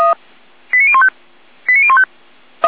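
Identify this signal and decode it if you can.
Five-tone selective call (ZVEI), heard as quick runs of five tones, tones of about 70 ms. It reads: 89015 89015
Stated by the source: ZVEI five-tone selective call 89068 received on an SDR.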